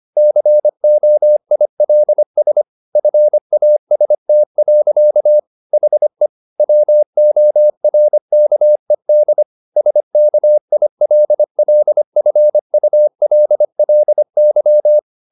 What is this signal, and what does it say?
Morse code, heard as a keyed tone, about 25 wpm, 600 Hz. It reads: COILS FAST. HE WORKED SKILLFULLY
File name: the old man and the sea part 1 25wpm